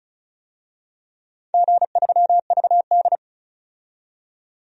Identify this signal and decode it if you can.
Morse code, heard as a keyed tone, about 34 wpm, 700 Hz. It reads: G3VD
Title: DXCC-100 Call Signs - 4 Characters - 35wpm